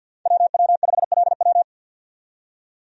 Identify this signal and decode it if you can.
Morse code, heard as a keyed tone, about 52 wpm, 700 Hz. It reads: WW5LW
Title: USA Call Signs - 5 Characters - 50wpm